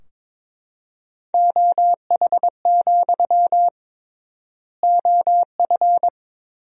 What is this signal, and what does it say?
Morse code, heard as a keyed tone, about 22 wpm, 700 Hz. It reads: OH, OF
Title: 3x Word-Spacing - Sentences from Top 500 Words - 22wpm